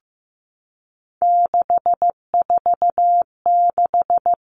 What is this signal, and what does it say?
Morse code, heard as a keyed tone, about 15 wpm, 700 Hz. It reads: 646
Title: Three Challenging Letter-Number Combinations 15wpm